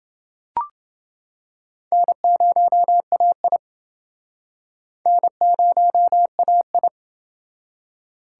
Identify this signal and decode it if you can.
Morse code, heard as a keyed tone, about 28 wpm, 700 Hz. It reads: N0AI N0AI